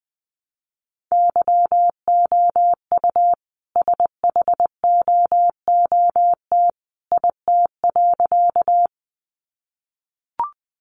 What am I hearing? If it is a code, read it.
Morse code, heard as a keyed tone, about 20 wpm, 700 Hz. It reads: YOU SHOOT IT.
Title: A Sound Of Thunder 20wpm